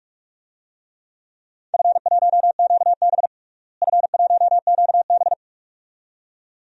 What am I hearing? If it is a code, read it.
Morse code, heard as a keyed tone, about 41 wpm, 700 Hz. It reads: F1QB F1QB
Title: DXCC-100 Call Signs - 4 Characters - Repeated 2x - 45wpm